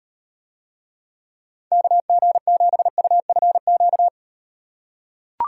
Morse code, heard as a keyed tone, about 36 wpm, 700 Hz. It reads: KG7UFQ